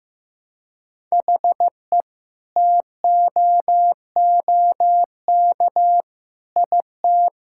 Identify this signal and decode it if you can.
Morse code, heard as a keyed tone, about 15 wpm, 700 Hz. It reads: HE TOOK IT
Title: Sentences from Top 1600 Words 15wpm